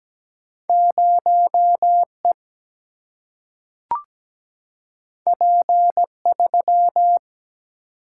Morse code, heard as a keyed tone, about 17 wpm, 700 Hz. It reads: 0E P3
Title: Single-Letter Single-Number 17wpm